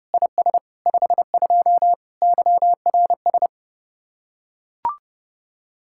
Morse code, heard as a keyed tone, about 30 wpm, 700 Hz. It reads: IS 52 YRS